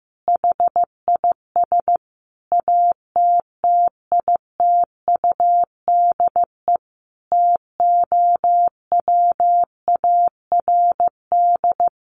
Morse code, heard as a keyed tone, about 15 wpm, 700 Hz. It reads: HIS ATTITUDE TOWARD
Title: Sentences from Top 1300 Words 15wpm